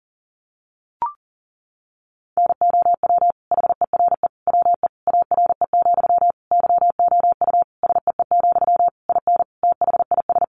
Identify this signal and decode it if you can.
Morse code, heard as a keyed tone, about 37 wpm, 700 Hz. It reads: NOW HERE WE ARE, YOU SEE, IN THIS
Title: Sentences from Top 1200 Words 40wpm